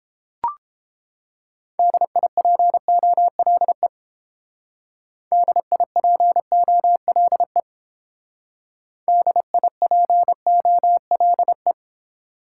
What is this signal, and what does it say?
Morse code, heard as a keyed tone, about 31 wpm, 700 Hz. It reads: DIPOLE DIPOLE DIPOLE